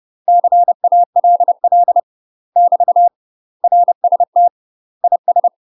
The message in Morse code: CALL <BT> RST IS

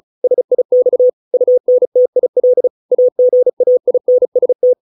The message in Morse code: SIX UNTIL AGAINST

